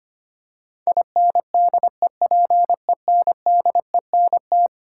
Morse code, heard as a keyed tone, about 25 wpm, 700 Hz. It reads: INDEPENDENT
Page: morse-code-ninja-practice-25wpm